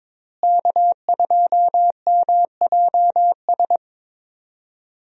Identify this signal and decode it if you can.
Morse code, heard as a keyed tone, about 22 wpm, 700 Hz. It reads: K2MJS
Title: USA Call Signs - 5 Characters - Repeated 3x - 22wpm